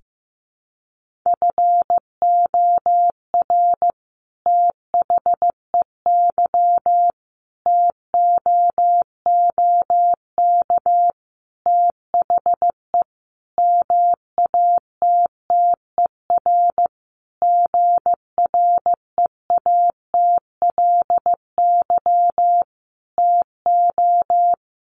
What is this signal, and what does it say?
Morse code, heard as a keyed tone, about 15 wpm, 700 Hz. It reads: FOR THEY TOOK THE MATTER GREATLY TO
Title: Sentences from Top 1400 Words 15wpm